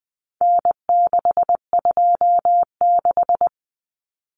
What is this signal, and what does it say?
Morse code, heard as a keyed tone, about 20 wpm, 700 Hz. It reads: N626
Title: Common Cut Numbers - 4-Digit 20wpm